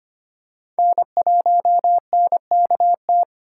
Morse code, heard as a keyed tone, about 25 wpm, 700 Hz. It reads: N1NKT
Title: USA Call Signs - 5 Characters - 3x Character Spacing - 25wpm